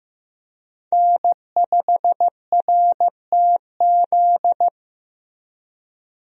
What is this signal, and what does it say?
Morse code, heard as a keyed tone, about 15 wpm, 700 Hz. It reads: N5RTZ
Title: USA Call Signs - 5 Characters - 15wpm